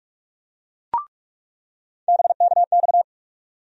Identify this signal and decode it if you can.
Morse code, heard as a keyed tone, about 46 wpm, 700 Hz. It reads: BKX
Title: USA Call Sign Postfixes - 3 Letters 45wpm